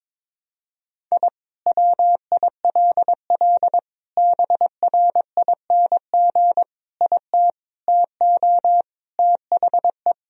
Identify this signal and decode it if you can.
Morse code, heard as a keyed tone, about 22 wpm, 700 Hz. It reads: I WILL BRING IT TO THE
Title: Sentences from Top 1000 Words - Review Part 3 - 22wpm